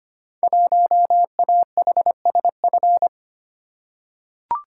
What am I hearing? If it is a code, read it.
Morse code, heard as a keyed tone, about 25 wpm, 700 Hz. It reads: 1AHSF
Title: Code Groups - 5 Letter:Numbers - 25wpm